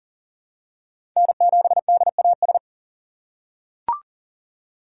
Morse code, heard as a keyed tone, about 40 wpm, 700 Hz. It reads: N7DAS
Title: USA Call Signs - 5 Characters - 4x Character Spacing - 40wpm